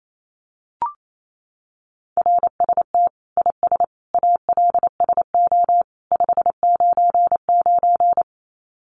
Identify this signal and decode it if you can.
Morse code, heard as a keyed tone, about 28 wpm, 700 Hz. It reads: RST IS ALSO 599
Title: Common QSO Phrases 28wpm